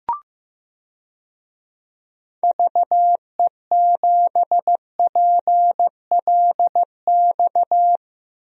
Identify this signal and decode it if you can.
Morse code, heard as a keyed tone, about 15 wpm, 700 Hz. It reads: VE7PLX
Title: DXCC-100 Call Signs - 6 Characters - Repeated 2x - 15wpm